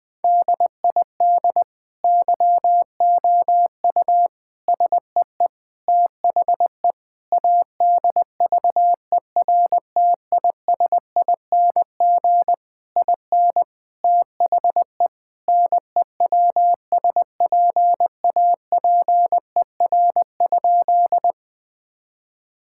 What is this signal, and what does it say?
Morse code, heard as a keyed tone, about 20 wpm, 700 Hz. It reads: DID YOU SEE THE ADVERTISING IN THE NEWSPAPER?
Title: Sentences from Top 1000 Words - Review Part 6 - 20wpm